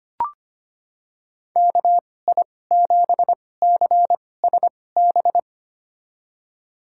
Morse code, heard as a keyed tone, about 25 wpm, spaced as for 18 wpm, 700 Hz. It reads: KI7CSB
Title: USA Call Signs - 6 Characters - 2x Character Spacing - 25wpm